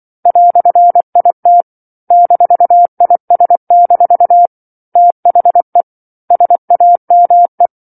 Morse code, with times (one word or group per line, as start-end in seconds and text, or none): 0.3-1.6 "IT
2.1-4.5 -IS-
5.0-5.8 THE
6.3-7.7 SAME